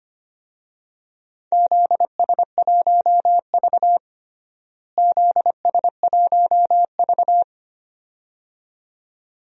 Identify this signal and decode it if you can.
Morse code, heard as a keyed tone, about 25 wpm, 700 Hz. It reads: ZS1V ZS1V
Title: DXCC-100 Call Signs - 4 Characters - Repeated 2x - 25wpm